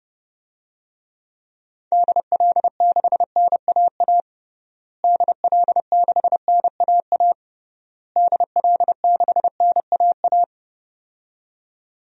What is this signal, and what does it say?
Morse code, heard as a keyed tone, about 30 wpm, 700 Hz. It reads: DL6NAA DL6NAA DL6NAA
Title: DXCC-100 Call Signs - 6 Characters - Repeated 3x - 30wpm